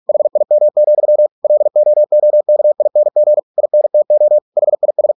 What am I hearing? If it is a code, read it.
Morse code, heard as a keyed tone, about 46 wpm, 600 Hz. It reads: HIM, LOOKING INTO HIS